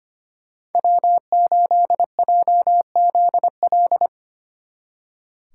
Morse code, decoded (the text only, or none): W8JZL